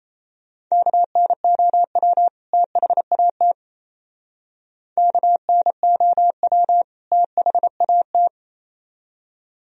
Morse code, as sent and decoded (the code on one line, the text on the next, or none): -.- -. --- .-- / - .... .- - / -.- -. --- .-- / - .... .- -
KNOW THAT KNOW THAT